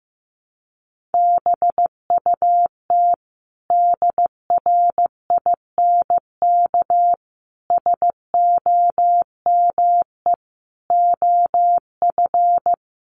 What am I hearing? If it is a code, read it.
Morse code, heard as a keyed tone, about 15 wpm, 700 Hz. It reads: BUT DRINK SOME OF
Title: Sentences from Top 1500 Words 15wpm